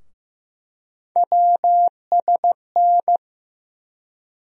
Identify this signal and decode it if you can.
Morse code, heard as a keyed tone, about 15 wpm, 700 Hz. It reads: WSN
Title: USA Call Sign Postfixes - 3 Letters 15wpm